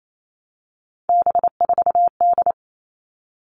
Morse code, heard as a keyed tone, about 28 wpm, 700 Hz. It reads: B4D
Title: All Cut Numbers - 3-Digit 28wpm